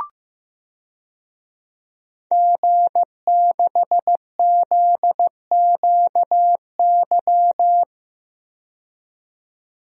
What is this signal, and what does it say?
Morse code, heard as a keyed tone, about 15 wpm, 700 Hz. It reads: G6ZQY
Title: DXCC-100 Call Signs - 5 Characters - 15wpm